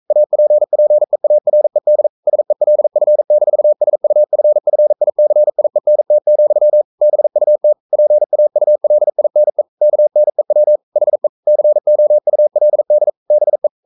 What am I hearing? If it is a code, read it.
Morse code, heard as a keyed tone, about 42 wpm, 600 Hz. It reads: APPEARED SELF-SUFFICIENT, BUT PAULINE KNEW HE COULD BE
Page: Romanen Never af Ken Follet på engelsk.